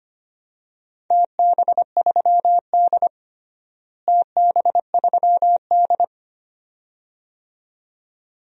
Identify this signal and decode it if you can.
Morse code, heard as a keyed tone, about 25 wpm, 700 Hz. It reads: TB3D TB3D
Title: DXCC-100 Call Signs - 4 Characters - Repeated 2x - 25wpm